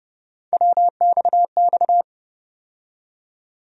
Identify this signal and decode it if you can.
Morse code, heard as a keyed tone, about 31 wpm, 700 Hz. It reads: WXX
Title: USA Call Sign Postfixes - 3 Letters 30wpm